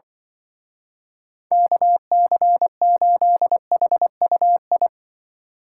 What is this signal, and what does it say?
Morse code, heard as a keyed tone, about 24 wpm, 700 Hz. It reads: KC8HUI
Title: Speed Racing - USA 2x3 Call Signs 20wpm